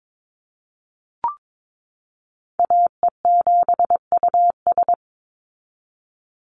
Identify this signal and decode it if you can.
Morse code, heard as a keyed tone, about 22 wpm, 700 Hz. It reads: AE7US
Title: Call Signs Repeated 3x - 22wpm